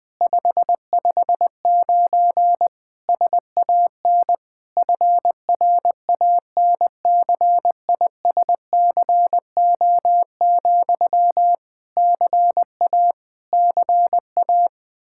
Morse code, heard as a keyed tone, about 20 wpm, 700 Hz. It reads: 559 SAN FRANCISCO, CA CA
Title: Practice QSO - Exchange 1 - Beginning - 20wpm